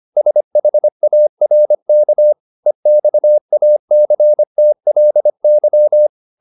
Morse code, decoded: SHARK EXACTLY

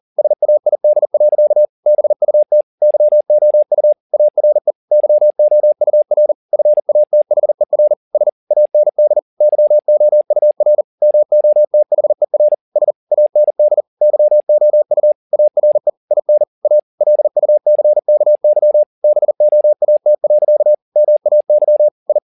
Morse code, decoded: SAID. BUT YOU ARE YOUR FATHER S AND YOUR MOTHER S AND YOU ARE IN A LUCKY BOAT. MAY I

